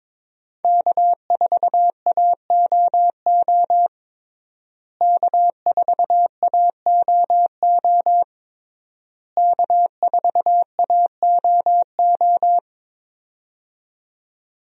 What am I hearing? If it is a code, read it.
Morse code, heard as a keyed tone, about 22 wpm, 700 Hz. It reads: K4AOO K4AOO K4AOO
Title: USA Call Signs - 5 Characters - Repeated 3x - 22wpm